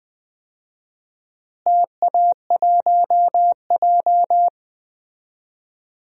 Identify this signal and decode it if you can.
Morse code, heard as a keyed tone, about 20 wpm, 700 Hz. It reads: TA1J
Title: DXCC-100 Call Signs - 4 Characters - Repeated 3x - 20wpm